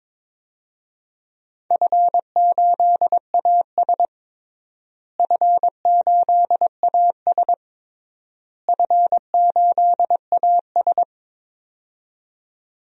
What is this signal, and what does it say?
Morse code, heard as a keyed tone, about 22 wpm, 700 Hz. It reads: F8AS F8AS F8AS
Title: DXCC-100 Call Signs - 4 Characters - Repeated 3x - 22wpm